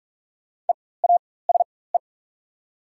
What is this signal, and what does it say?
Morse code, heard as a keyed tone, about 44 wpm, spaced as for 21 wpm, 700 Hz. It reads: EASE
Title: 4x Character Spacing - Top 500 Words - 45wpm